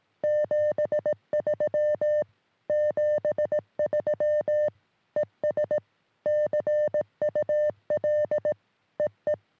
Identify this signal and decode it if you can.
Morse code, heard as a keyed tone, about 18 wpm, 600 Hz. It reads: 73 73 ES CUL EE
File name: sm7lcb / ok2poi 3cm